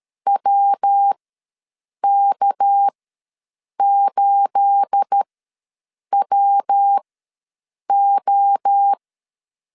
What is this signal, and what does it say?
Morse code, heard as a keyed tone, about 12 wpm, spaced as for 7 wpm, 800 Hz. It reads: WK8WO
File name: LEZIONI / 9 2867 ws 12wpm 3431